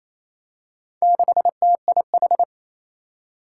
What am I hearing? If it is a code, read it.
Morse code, heard as a keyed tone, about 27 wpm, 700 Hz. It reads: 6TIH
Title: Four Challenging Letter-Number Combinations 28wpm